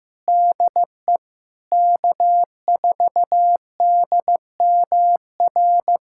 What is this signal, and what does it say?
Morse code, heard as a keyed tone, about 15 wpm, 700 Hz. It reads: DE K4DMR
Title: USA Callsign DE Callsign - 5 Characters - 15wpm